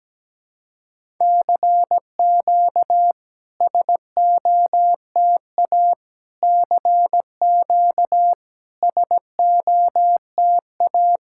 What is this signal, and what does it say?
Morse code, heard as a keyed tone, about 17 wpm, 700 Hz. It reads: CQ SOTA CQ SOTA
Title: SOTA QSO Practice 17wpm